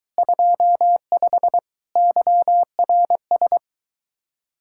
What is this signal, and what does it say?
Morse code, heard as a keyed tone, about 24 wpm, 700 Hz. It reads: 25 YRS